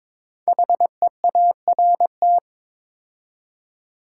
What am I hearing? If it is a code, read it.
Morse code, heard as a keyed tone, about 22 wpm, 700 Hz. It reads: HEART
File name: 5 Letter Words - First 3 Letters Given - 22wpm